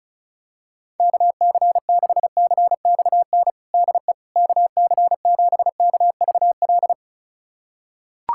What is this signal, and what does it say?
Morse code, heard as a keyed tone, about 34 wpm, 700 Hz. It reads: KC6CXN DE KC7KVL
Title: USA Callsign DE Callsign - 6 Characters - 35wpm